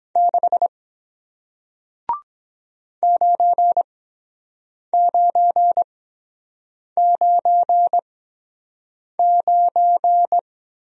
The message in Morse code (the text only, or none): 6 9999